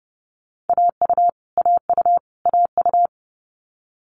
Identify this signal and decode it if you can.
Morse code, heard as a keyed tone, about 30 wpm, 700 Hz. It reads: AU AU AU
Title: Multi-letter Phonemes Repeated 3x 30wpm